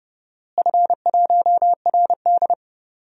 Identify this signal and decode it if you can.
Morse code, heard as a keyed tone, about 30 wpm, 700 Hz. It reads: F1RD
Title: DXCC-100 Call Signs - 4 Characters - Repeated 3x - 30wpm